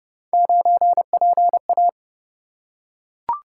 Morse code, heard as a keyed tone, about 29 wpm, 700 Hz. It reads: 9PA